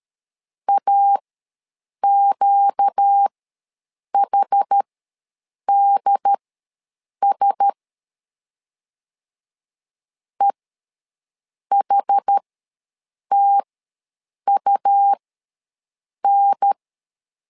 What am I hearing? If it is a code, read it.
Morse code, heard as a keyed tone, about 13 wpm, spaced as for 7 wpm, 800 Hz. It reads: AQHDS EHTUN